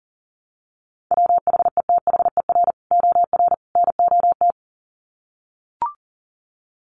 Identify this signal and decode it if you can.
Morse code, heard as a keyed tone, about 37 wpm, 700 Hz. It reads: WHETHER OR NOT
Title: Sets of 3 Words 40wpm